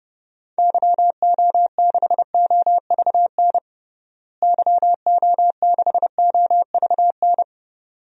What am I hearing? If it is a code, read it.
Morse code, heard as a keyed tone, about 29 wpm, 700 Hz. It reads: YO6OVN YO6OVN
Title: DXCC-100 Call Signs - 6 Characters - Repeated 2x - 30wpm